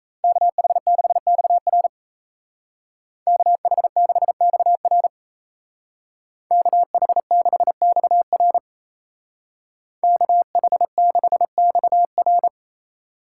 Morse code, heard as a keyed tone, about 34 wpm, 700 Hz. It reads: KH6XR KH6XR KH6XR KH6XR
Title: Speed Racing - USA 2x2 Call Signs 28wpm